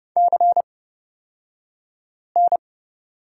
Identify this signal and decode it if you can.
Morse code, heard as a keyed tone, about 31 wpm, spaced as for 5 wpm, 700 Hz. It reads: CN
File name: Lesson 37d - Letter x - Two Characters 30wpm